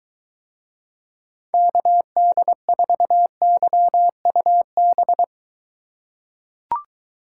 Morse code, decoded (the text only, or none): KD4YUB